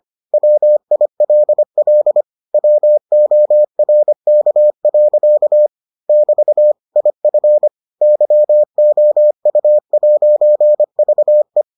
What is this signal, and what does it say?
Morse code, heard as a keyed tone, about 25 wpm, 600 Hz. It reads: WILL WORK. <BT> IF YOU'VE